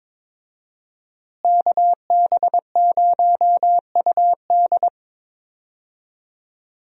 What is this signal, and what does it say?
Morse code, heard as a keyed tone, about 22 wpm, 700 Hz. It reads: KB0UD
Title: USA Call Signs - 5 Characters - Repeated 3x - 22wpm